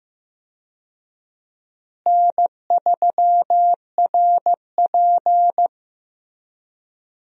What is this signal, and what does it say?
Morse code, heard as a keyed tone, about 15 wpm, 700 Hz. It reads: N3RP
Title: USA Call Signs - 4 Characters - Repeated 2x - 15wpm